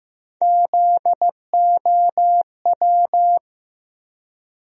Morse code, heard as a keyed tone, about 16 wpm, 700 Hz. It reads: ZOW
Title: USA Call Sign Postfixes - 3 Letters 15wpm